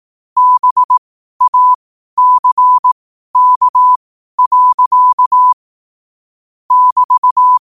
Morse code, heard as a keyed tone, about 18 wpm, spaced as for 12 wpm, 1000 Hz. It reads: BACK. <BT>
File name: qotdfiles / 12wpm